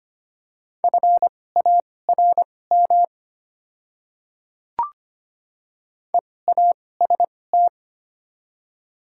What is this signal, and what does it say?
Morse code, heard as a keyed tone, about 25 wpm, spaced as for 18 wpm, 700 Hz. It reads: FARM EAST
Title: 2x Character Spacing - Top 500 Words - 25wpm